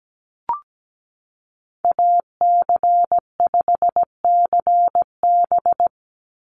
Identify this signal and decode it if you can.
Morse code, heard as a keyed tone, about 17 wpm, 700 Hz. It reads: AC5CB
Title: USA Call Signs 17wpm